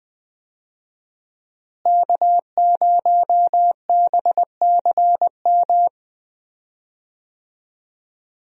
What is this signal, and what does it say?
Morse code, heard as a keyed tone, about 20 wpm, 700 Hz. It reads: K0BCM
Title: USA Call Signs - 5 Characters - Repeated 2x - 20wpm